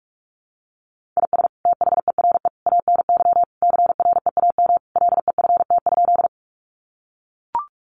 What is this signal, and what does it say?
Morse code, heard as a keyed tone, about 44 wpm, 700 Hz. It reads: IS THERE ANY CREAM LEFT?
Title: Sentences from Top 1500 Words 45wpm